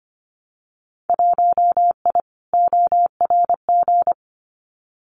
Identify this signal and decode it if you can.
Morse code, heard as a keyed tone, about 25 wpm, 700 Hz. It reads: 1I ORG
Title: Winter Field Day Exchanges Encore 25wpm